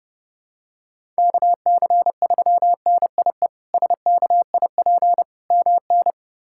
Morse code, heard as a keyed tone, about 29 wpm, 700 Hz. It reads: KC3NIE SKIP MN
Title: K1USN SST QSO Practice - 30wpm